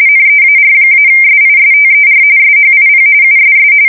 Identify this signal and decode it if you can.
RTTY, about 45.45 baud. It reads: NEAR-SISTER SHIP USS
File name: fsk170 / battleships